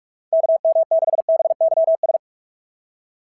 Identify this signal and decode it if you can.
Morse code, heard as a keyed tone, about 44 wpm, 660 Hz. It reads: KM6BYS
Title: SOTA QSO Practice 45wpm